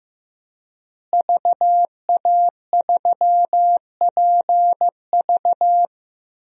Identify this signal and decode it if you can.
Morse code, heard as a keyed tone, about 16 wpm, 700 Hz. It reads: VA3PV
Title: DXCC-100 Call Signs - 5 Characters - Repeated 3x - 15wpm